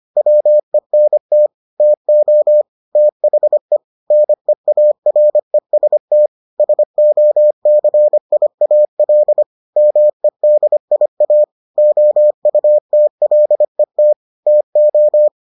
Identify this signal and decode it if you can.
Morse code, heard as a keyed tone, about 25 wpm, 600 Hz. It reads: WENT TO THE NEAREST SOCIAL MEDIA OUTLET TO